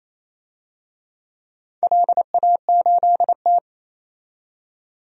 Morse code, heard as a keyed tone, about 29 wpm, 690 Hz. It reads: LA8T
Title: DXCC-100 Call Signs - 4 Characters - 28wpm